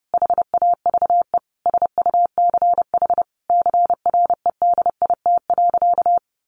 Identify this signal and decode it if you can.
Morse code, heard as a keyed tone, about 30 wpm, 700 Hz. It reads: HAVE SUCH CREDIT.
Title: Sentences from Top 1000 Words 30wpm